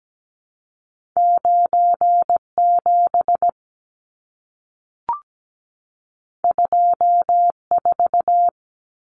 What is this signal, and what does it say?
Morse code, heard as a keyed tone, about 17 wpm, 700 Hz. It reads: 97 24
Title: Numbers - 2-Digit 17wpm